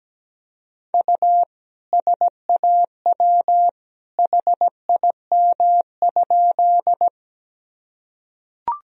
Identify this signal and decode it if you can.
Morse code, heard as a keyed tone, about 17 wpm, 700 Hz. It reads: U SAW HIM?
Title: Sentences from Top 300 Words and Abbreviations 17wpm